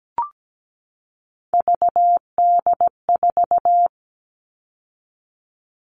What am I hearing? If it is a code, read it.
Morse code, heard as a keyed tone, about 17 wpm, 700 Hz. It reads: VD4
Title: All Cut Numbers - 3-Digit 17wpm